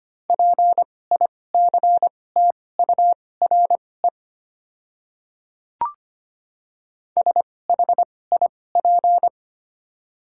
Morse code, heard as a keyed tone, about 25 wpm, spaced as for 18 wpm, 690 Hz. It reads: PICTURE SHIP
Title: 2x Character Spacing - Top 500 Words - 25wpm